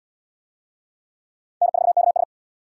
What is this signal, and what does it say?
Morse code, heard as a keyed tone, about 67 wpm, 700 Hz. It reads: N4KS